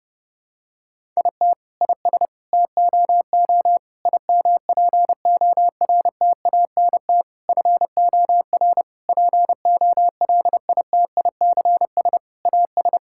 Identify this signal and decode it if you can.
Morse code, heard as a keyed tone, about 30 wpm, 700 Hz. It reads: IT IS TOO IMPORTANT FOR POLITICS AS
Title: Sentences from Top 1000 Words - Review Part 6 - 30wpm